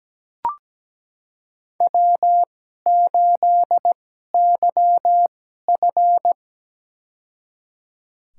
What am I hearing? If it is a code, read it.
Morse code, heard as a keyed tone, about 17 wpm, spaced as for 12 wpm, 700 Hz. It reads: W8YF
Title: USA Call Signs - 4 Characters - 2x Character Spacing - 17wpm